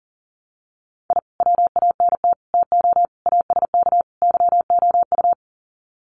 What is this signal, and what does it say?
Morse code, heard as a keyed tone, about 40 wpm, 700 Hz. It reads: I WANT TO ASK YOU
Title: Sets of 5 Words 40wpm